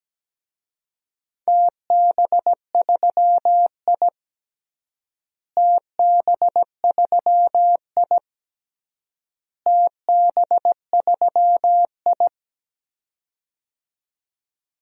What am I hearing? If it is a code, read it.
Morse code, heard as a keyed tone, about 17 wpm, 700 Hz. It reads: TB3I TB3I TB3I
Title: DXCC-100 Call Signs - 4 Characters - Repeated 3x - 17wpm